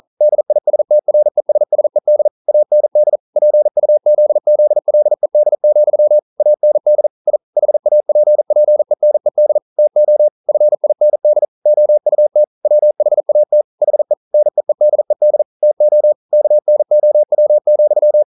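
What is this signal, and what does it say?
Morse code, heard as a keyed tone, about 40 wpm, 600 Hz. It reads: DISTRESSED AND PUZZLED, AND I HAPPENED TO FIND OUT WHAT HE NEEDED TO KNOW,
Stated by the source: Morse code practice at 41 wpm, from CW med Gnister.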